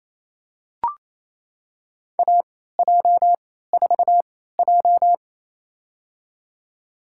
Morse code, heard as a keyed tone, about 28 wpm, spaced as for 16 wpm, 700 Hz. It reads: AJ4J
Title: USA Call Signs - 4 Characters - 3x Character Spacing - 28wpm